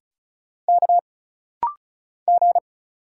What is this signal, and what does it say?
Morse code, heard as a keyed tone, about 36 wpm, 700 Hz. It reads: K G
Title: Single Letter-Number Alt - Rapid-Fire 35wpm